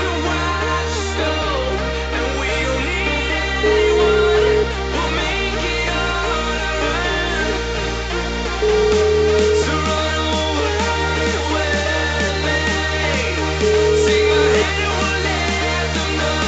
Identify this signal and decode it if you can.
Ringback tone (Europe, 425 Hz), with noise.